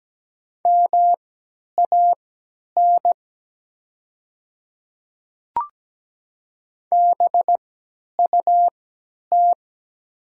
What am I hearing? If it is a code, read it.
Morse code, heard as a keyed tone, about 17 wpm, spaced as for 10 wpm, 690 Hz. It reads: MAN BUT
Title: 3x Character Spacing - Top 300 Words - 17wpm